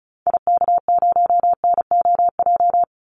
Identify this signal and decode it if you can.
Morse code, heard as a keyed tone, about 34 wpm, 700 Hz. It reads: IK0NOJ
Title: Call Signs Repeated 2x - 35wpm